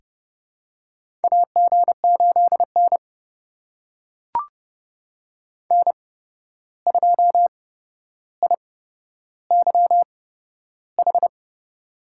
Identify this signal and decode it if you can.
Morse code, heard as a keyed tone, about 30 wpm, spaced as for 12 wpm, 700 Hz. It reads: AG8N N2IYH
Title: USA Call Signs - 8x Character Spacing - 30wpm